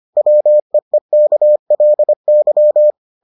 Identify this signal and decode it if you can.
Morse code, heard as a keyed tone, about 25 wpm, 600 Hz. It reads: WEEKLY